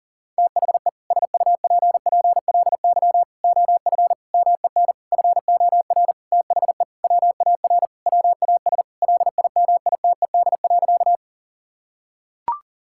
Morse code, heard as a keyed tone, about 40 wpm, 700 Hz. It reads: THE SUPPLY OF MEN FOR THE WAR WAS LIMITED.
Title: Sentences from Top 1000 Words - Review Part 6 - 40wpm